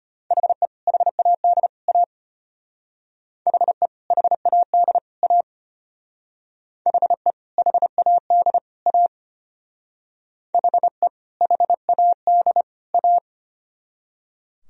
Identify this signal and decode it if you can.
Morse code, heard as a keyed tone, about 32 wpm, 700 Hz. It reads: HE HAD A HE HAD A HE HAD A HE HAD A